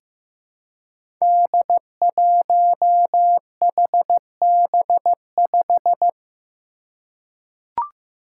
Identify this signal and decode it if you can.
Morse code, heard as a keyed tone, about 15 wpm, 700 Hz. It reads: D1HB5
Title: Five Challenging Letter-Number Combinations 15wpm